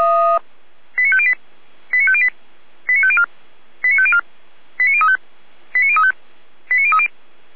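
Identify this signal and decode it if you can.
Five-tone selective call (ZVEI), heard as quick runs of five tones, tones of about 70 ms. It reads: 89508 89508 89504 89504 89035 89035 89030
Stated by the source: Five-tone ZVEI selective call received by SDR, from server of Feuerwehr Orlamünde (fire brigade).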